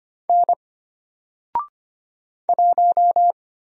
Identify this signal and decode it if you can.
Morse code, heard as a keyed tone, about 25 wpm, 700 Hz. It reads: N 1
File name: Single-Letter-Number-25wpm